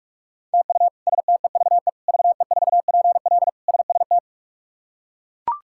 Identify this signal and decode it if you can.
Morse code, heard as a keyed tone, about 42 wpm, 700 Hz. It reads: TU STEVE VE4PL SST
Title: K1USN SST QSO Practice - 45wpm